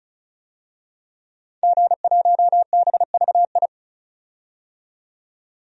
Morse code, heard as a keyed tone, about 36 wpm, 700 Hz. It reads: G1BVI
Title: DXCC-100 Call Signs - 5 Characters - 35wpm